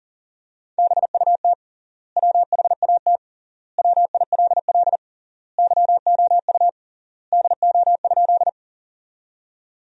Morse code, heard as a keyed tone, about 37 wpm, 700 Hz. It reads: BUT WHAT WILL YOU DO?
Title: 3x Word-Spacing - Sentences from Top 100 Words - 40wpm